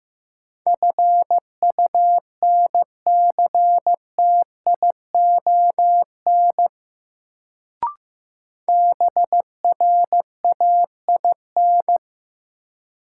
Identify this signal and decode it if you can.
Morse code, heard as a keyed tone, about 15 wpm, 700 Hz. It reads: FUNCTION BRAIN
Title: Sentences from Top 1100 Words - New Word Drill Alt - 15wpm